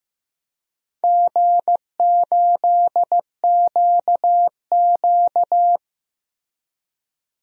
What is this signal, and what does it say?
Morse code, heard as a keyed tone, about 15 wpm, 700 Hz. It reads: G8QQ
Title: DXCC-100 Call Signs - 4 Characters - Repeated 2x - 15wpm